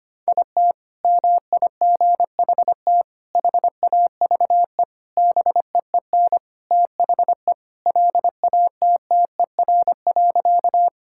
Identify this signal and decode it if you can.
Morse code, heard as a keyed tone, about 25 wpm, 700 Hz. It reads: IT MIGHT HAVE BEEN THE LATTER.
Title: Sentences from Top 1600 Words 25wpm